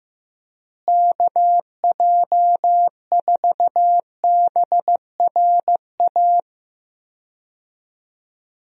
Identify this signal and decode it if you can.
Morse code, heard as a keyed tone, about 16 wpm, 700 Hz. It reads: KJ4BRA
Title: USA Call Signs - 6 Characters - Repeated 3x - 15wpm